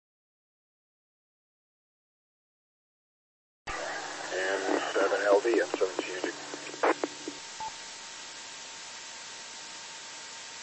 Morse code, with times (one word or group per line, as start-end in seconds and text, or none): none